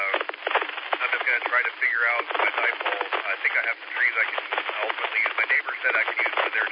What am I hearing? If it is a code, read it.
Emergency Alert System (SAME) tones (header bursts), heard as harsh buzzing bursts, with speech. It is heard from 1.1 to 6.3 s.